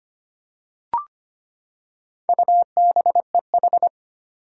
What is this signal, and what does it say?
Morse code, heard as a keyed tone, about 25 wpm, 700 Hz. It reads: UBEH
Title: Four Challenging Letter-Number Combinations 25wpm